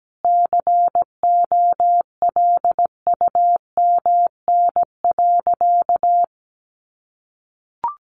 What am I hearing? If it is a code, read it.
Morse code, heard as a keyed tone, about 17 wpm, 700 Hz. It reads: COLUMN.
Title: A Sound Of Thunder 17wpm